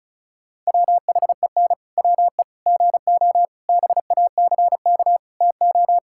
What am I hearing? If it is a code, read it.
Morse code, heard as a keyed tone, about 34 wpm, 700 Hz. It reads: WHEN WE GO BACK TO
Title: Pangram Challenge v2 - 35wpm